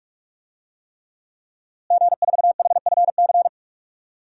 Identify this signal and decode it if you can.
Morse code, heard as a keyed tone, about 46 wpm, 700 Hz. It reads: G4HFC
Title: DXCC-100 Call Signs - 5 Characters - Repeated 3x - 45wpm